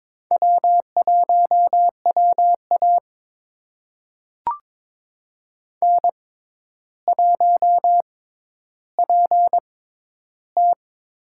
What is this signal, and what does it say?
Morse code, heard as a keyed tone, about 22 wpm, spaced as for 11 wpm, 700 Hz. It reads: W1WA N1PT
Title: USA Call Signs - 4 Characters - 6x Character Spacing - 22wpm